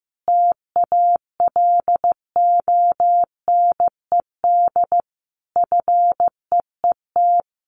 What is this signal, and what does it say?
Morse code, heard as a keyed tone, about 15 wpm, 700 Hz. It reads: TALONED FEET
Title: A Sound Of Thunder 15wpm